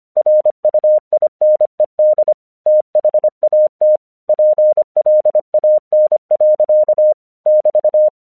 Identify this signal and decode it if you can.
Morse code, heard as a keyed tone, about 25 wpm, 610 Hz. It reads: RUINED THAT PLAN. <BT>
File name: 1 English 25wpm / AQC15